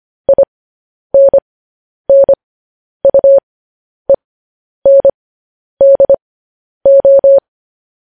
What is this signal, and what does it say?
Morse code, heard as a keyed tone, about 25 wpm, spaced as for 10 wpm, 550 Hz. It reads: INNUENDO
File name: podcast / News headlines-1x-25x10